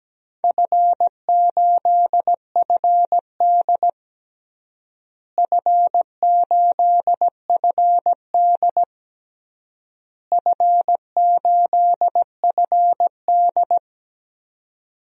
Morse code, heard as a keyed tone, about 17 wpm, 700 Hz. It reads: F8FD F8FD F8FD
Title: DXCC-100 Call Signs - 4 Characters - Repeated 3x - 17wpm